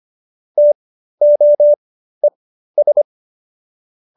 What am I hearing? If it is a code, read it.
Morse code, heard as a keyed tone, about 25 wpm, spaced as for 13 wpm, 590 Hz. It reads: TOES